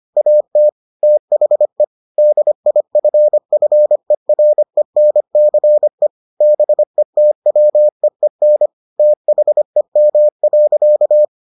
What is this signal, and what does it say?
Morse code, heard as a keyed tone, about 25 wpm, 600 Hz. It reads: AT THE DIFFERENCE BETWEEN THEM.